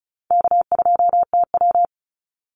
Morse code, heard as a keyed tone, about 34 wpm, 700 Hz. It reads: K2TW
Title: Call Signs - 35wpm